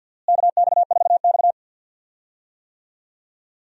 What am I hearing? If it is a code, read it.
Morse code, heard as a keyed tone, about 47 wpm, 700 Hz. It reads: KX4X
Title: USA Call Signs - 4 Characters - Repeated 3x - 50wpm